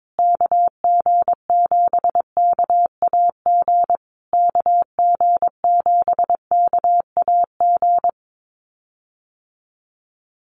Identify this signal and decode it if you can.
Morse code, heard as a keyed tone, about 22 wpm, 700 Hz. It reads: KG7KAG KG7KAG
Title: USA Call Signs Repeated 3x 22wpm